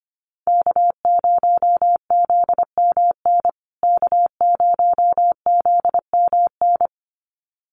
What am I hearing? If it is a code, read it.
Morse code, heard as a keyed tone, about 25 wpm, 700 Hz. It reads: K0ZMN K0ZMN
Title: USA Call Signs Repeated 2x 25wpm